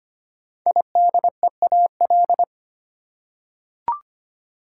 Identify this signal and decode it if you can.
Morse code, heard as a keyed tone, about 25 wpm, 700 Hz. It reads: IDEAL
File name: 5 Letter Words - First 3 Letters Given - 25wpm